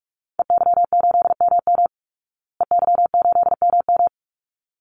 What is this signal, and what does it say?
Morse code, heard as a keyed tone, about 44 wpm, 700 Hz. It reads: EY8MM EY8MM
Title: Call Signs Repeated 3x - 45wpm